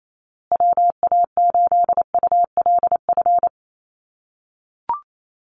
Morse code, heard as a keyed tone, about 28 wpm, 700 Hz. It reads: WA8ULF
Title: USA Call Signs Repeated 2x 28wpm